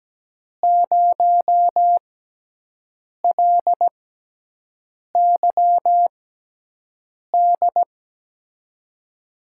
Morse code, heard as a keyed tone, about 17 wpm, spaced as for 6 wpm, 700 Hz. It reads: 0LYD
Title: USA Call Signs - 6 Characters - 6x Character Spacing - 17wpm